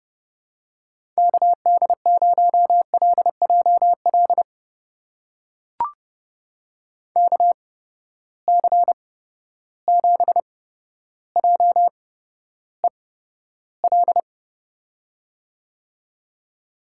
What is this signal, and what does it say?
Morse code, heard as a keyed tone, about 30 wpm, spaced as for 13 wpm, 700 Hz. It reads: KD0LJL KC7JEL
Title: USA Call Signs - 6 Characters - 8x Character Spacing - 30wpm